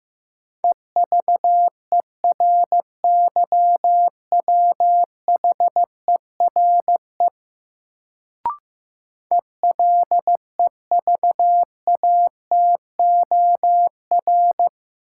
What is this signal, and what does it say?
Morse code, heard as a keyed tone, about 15 wpm, 700 Hz. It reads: EVERYWHERE ELEVATOR
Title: Sentences from Top 1600 Words - New Word Drill - 15wpm